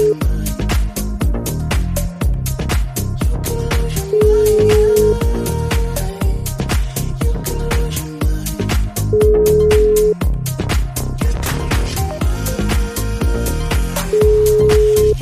Ringback tone (Europe, 425 Hz).